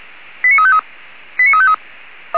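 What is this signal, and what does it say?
Five-tone selective call (ZVEI), heard as quick runs of five tones, tones of about 70 ms. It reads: 89353 89353
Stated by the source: ZVEI five-tone selective call received by SDR.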